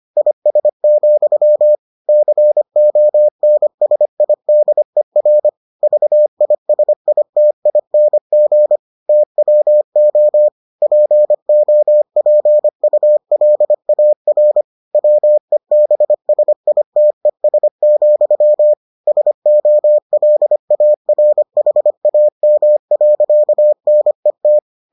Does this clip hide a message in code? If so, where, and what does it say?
Yes: Morse code from 0.2 to 24.6 s, heard as a keyed tone, about 25 wpm, 600 Hz. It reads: IS, CONSIDER VISITING TWO POPULAR WEBSITES, SOLARHAM.NET